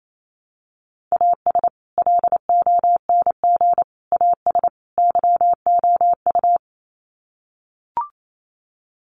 Morse code, heard as a keyed tone, about 28 wpm, 700 Hz. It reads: AS LONG AS YOU
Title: Sets of 4 Words 28wpm